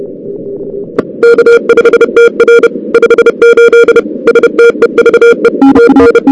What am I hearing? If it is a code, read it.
Morse code, heard as a keyed tone, about 31 wpm, 470 Hz, with noise. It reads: K5TR 58 STEVE TN
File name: CWSprint / Feb2006 / 2006-feb-cw-sprint-k5tr op-n5rz -02:00